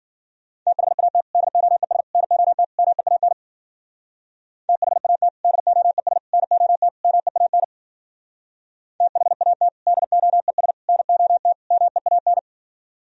Morse code, heard as a keyed tone, about 51 wpm, 700 Hz. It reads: THAT DOES NOT MEAN THAT DOES NOT MEAN THAT DOES NOT MEAN